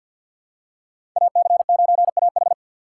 Morse code, heard as a keyed tone, about 47 wpm, 700 Hz. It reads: AC9AH
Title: USA Call Signs - 6x Character Spacing - 50wpm